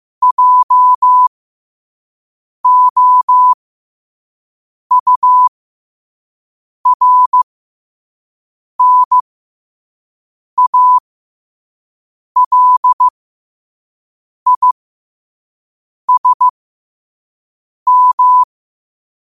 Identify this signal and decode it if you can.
Morse code, heard as a keyed tone, about 15 wpm, spaced as for 5 wpm, 1000 Hz. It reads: JOURNALISM